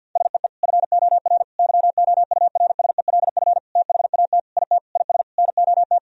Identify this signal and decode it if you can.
Morse code, heard as a keyed tone, about 52 wpm, 700 Hz. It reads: SEE FOR YOURSELF THAT IT IS NOT
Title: Sentences from Top 1000 Words - Review Part 1 - 50wpm